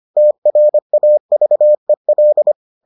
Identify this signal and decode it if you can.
Morse code, heard as a keyed tone, about 25 wpm, 600 Hz. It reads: TRAVEL